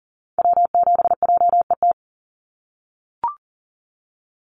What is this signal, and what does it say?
Morse code, heard as a keyed tone, about 40 wpm, 700 Hz. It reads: W7JET
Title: NA SOTA Activator Callsigns 40wpm